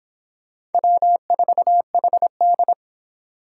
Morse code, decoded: W4HD

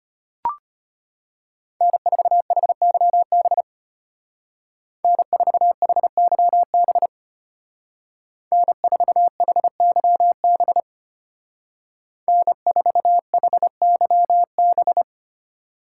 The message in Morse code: N4HYB N4HYB N4HYB N4HYB

